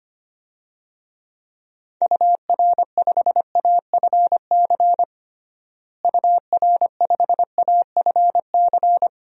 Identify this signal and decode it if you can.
Morse code, heard as a keyed tone, about 25 wpm, 700 Hz. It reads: UR5AFC UR5AFC
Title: DXCC-100 Call Signs - 6 Characters - Repeated 2x - 25wpm